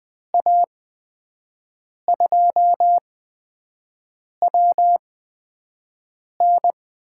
Morse code, heard as a keyed tone, about 20 wpm, spaced as for 5 wpm, 700 Hz. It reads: A2WN
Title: USA Call Signs - 6 Characters - 8x Character Spacing - 20wpm